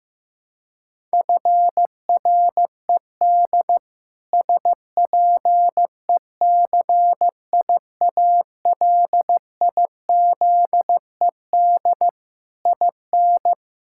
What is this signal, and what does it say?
Morse code, heard as a keyed tone, about 15 wpm, 700 Hz. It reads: FRED SPECIALIZED IN
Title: Pangram Challenge v2 - 15wpm